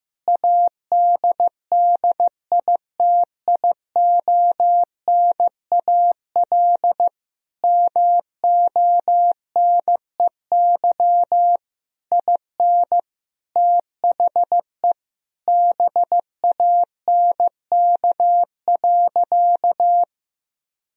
Morse code, heard as a keyed tone, about 15 wpm, 700 Hz. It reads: ADDITIONAL MONEY IN THE BANK.
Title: Sentences from Top 1000 Words - Review Part 4 - 15wpm